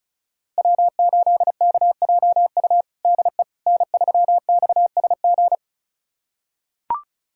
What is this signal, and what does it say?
Morse code, heard as a keyed tone, about 36 wpm, 700 Hz. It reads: W8KJU DE N3XSG
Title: USA Callsign DE Callsign - 5 Characters - 35wpm